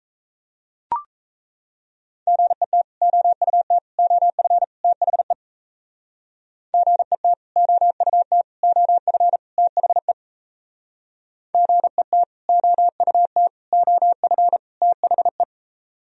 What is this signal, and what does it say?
Morse code, heard as a keyed tone, about 35 wpm, 700 Hz. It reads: GET OUT OF THE GET OUT OF THE GET OUT OF THE